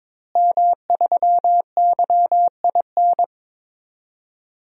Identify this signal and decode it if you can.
Morse code, heard as a keyed tone, about 22 wpm, 700 Hz. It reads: M3YIN
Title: DXCC-100 Call Signs - 5 Characters - Repeated 2x - 22wpm